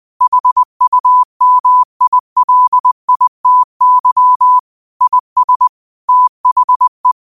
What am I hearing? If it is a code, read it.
Morse code, heard as a keyed tone, about 20 wpm, 1000 Hz. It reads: HUMILITY IS THE